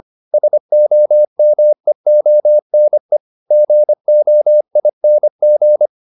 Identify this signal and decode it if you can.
Morse code, heard as a keyed tone, about 25 wpm, 600 Hz. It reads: SOMEONE GOING